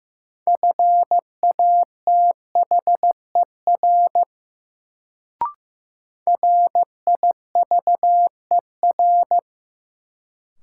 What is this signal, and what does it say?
Morse code, heard as a keyed tone, about 15 wpm, 700 Hz. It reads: FATHER RIVER
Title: Sentences from Top 300 Words - New Word Drill - 15wpm